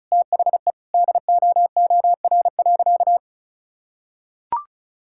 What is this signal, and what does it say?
Morse code, heard as a keyed tone, about 36 wpm, 700 Hz. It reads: THE DOOR.
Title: Sentences from Top 1000 Words - Review Part 6 - 35wpm